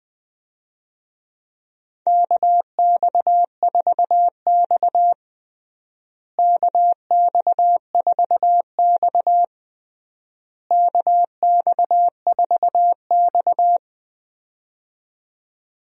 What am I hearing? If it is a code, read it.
Morse code, heard as a keyed tone, about 20 wpm, 700 Hz. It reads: KX4X KX4X KX4X
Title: USA Call Signs - 4 Characters - Repeated 3x - 20wpm